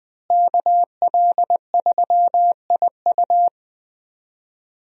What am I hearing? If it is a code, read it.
Morse code, heard as a keyed tone, about 20 wpm, 700 Hz. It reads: KL3IU
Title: USA Call Signs - 5 Characters - Repeated 3x - 20wpm